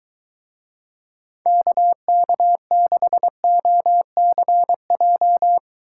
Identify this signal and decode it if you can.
Morse code, heard as a keyed tone, about 22 wpm, 700 Hz. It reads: KK6OCJ